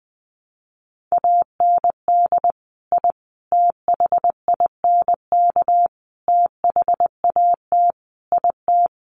Morse code, heard as a keyed tone, about 20 wpm, 700 Hz. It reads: AND I THINK THAT IT
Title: Sets of 5 Words Encore2 20wpm